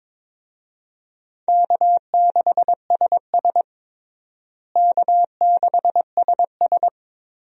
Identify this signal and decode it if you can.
Morse code, heard as a keyed tone, about 22 wpm, 700 Hz. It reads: K6SS K6SS
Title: USA Call Signs - 4 Characters - Repeated 2x - 22wpm